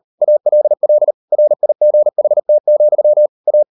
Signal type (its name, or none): Morse code